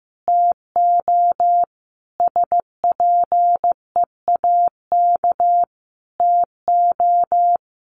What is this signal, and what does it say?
Morse code, heard as a keyed tone, about 15 wpm, 700 Hz. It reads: TO SPEAK TO
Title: Sentences from Top 1100 Words 15wpm